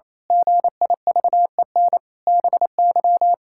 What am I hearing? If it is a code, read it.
Morse code, heard as a keyed tone, about 28 wpm, 700 Hz. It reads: GIVEN BY
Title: Pangram Challenge v2 - 28wpm